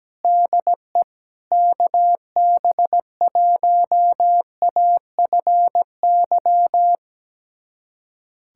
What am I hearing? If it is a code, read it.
Morse code, heard as a keyed tone, about 17 wpm, 700 Hz. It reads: DE KB1AFY
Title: USA Callsign DE Callsign - 6 Characters - 17wpm